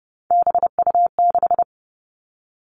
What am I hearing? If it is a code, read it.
Morse code, heard as a keyed tone, about 30 wpm, 700 Hz. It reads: BU6
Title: Three Challenging Letter-Number Combinations 30wpm